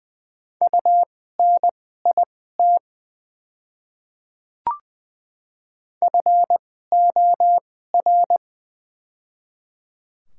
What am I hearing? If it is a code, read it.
Morse code, heard as a keyed tone, about 21 wpm, spaced as for 15 wpm, 700 Hz. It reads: UNIT FOR
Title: 2x Character Spacing - Top 500 Words - 20wpm